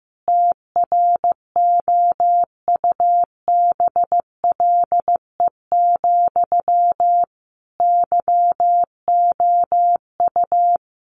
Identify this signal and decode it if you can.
Morse code, heard as a keyed tone, about 15 wpm, 700 Hz. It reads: TROUBLE, YOU
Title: Sentences from Top 1200 Words 15wpm